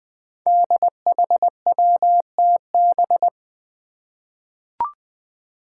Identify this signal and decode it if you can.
Morse code, heard as a keyed tone, about 20 wpm, 700 Hz. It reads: DHWTB
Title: Five Challenging Letter-Number Combinations 20wpm